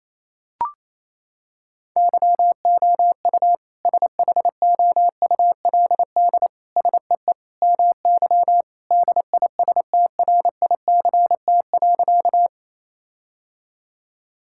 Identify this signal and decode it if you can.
Morse code, heard as a keyed tone, about 28 wpm, 700 Hz. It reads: YOU SHOULD SEE MY DISTRICT.
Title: Sentences from Top 1600 Words 28wpm